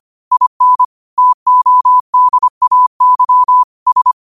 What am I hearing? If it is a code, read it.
Morse code, heard as a keyed tone, about 25 wpm, 1000 Hz. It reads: IN TODAY S
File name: qotdfiles / 25wpm